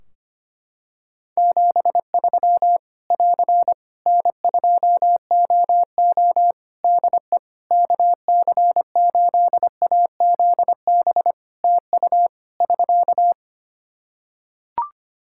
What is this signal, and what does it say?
Morse code, heard as a keyed tone, about 25 wpm, 700 Hz. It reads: 73 <AR> N2OO DE KC8AZB TU <SK>
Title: Practice QSO - Exchange 4 - Ending - 25wpm